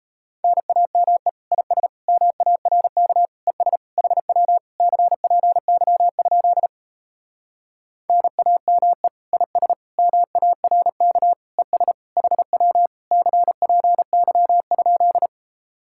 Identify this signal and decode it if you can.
Morse code, heard as a keyed tone, about 38 wpm, 700 Hz. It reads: NAME IS MARK ES HW CPY? NAME IS MARK ES HW CPY?